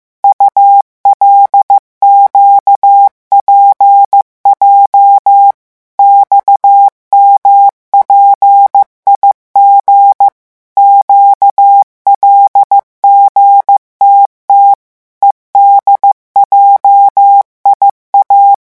Morse code, heard as a keyed tone, about 15 wpm, 800 Hz. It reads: ULQPJ XMPIG QLGTT EDJIA
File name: morse mp3 / revision7-vitesse 14 mots